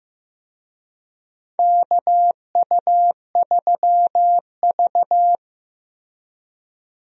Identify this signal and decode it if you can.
Morse code, heard as a keyed tone, about 15 wpm, 700 Hz. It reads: KU3V
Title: USA Call Signs - 4 Characters - Repeated 2x - 15wpm